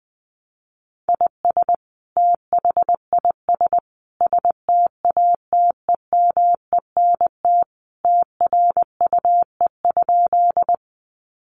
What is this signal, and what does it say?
Morse code, heard as a keyed tone, about 20 wpm, 700 Hz. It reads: IS THIS STATEMENT TRUE?
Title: Sentences from Top 1200 Words 20wpm